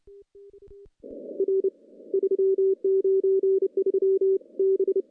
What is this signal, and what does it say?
Morse code, heard as a keyed tone, about 28 wpm, 380 Hz. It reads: R 393 B
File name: contests / CW-SS-k5tr- -0800